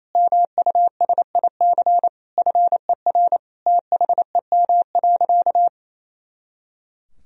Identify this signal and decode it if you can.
Morse code, heard as a keyed tone, about 28 wpm, 700 Hz. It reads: MUSIC FER THEM.
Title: Sentences from Top 300 Words and Abbreviations 28wpm